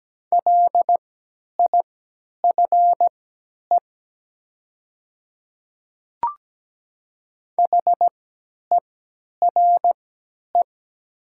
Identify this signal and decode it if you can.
Morse code, heard as a keyed tone, about 17 wpm, spaced as for 10 wpm, 700 Hz. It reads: LIFE HERE
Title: 3x Character Spacing - Top 300 Words - 17wpm